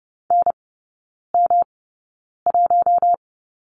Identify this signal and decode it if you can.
Morse code, heard as a keyed tone, about 30 wpm, spaced as for 9 wpm, 700 Hz. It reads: NM1
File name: Lesson 18e - Letter m - Three Characters 30wpm